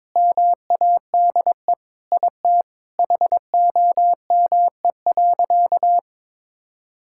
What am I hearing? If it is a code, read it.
Morse code, heard as a keyed tone, about 22 wpm, 700 Hz. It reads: MADE IT HOME.
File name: practice / 22wpm / Sentences with 4-Word Phrases - 22wpm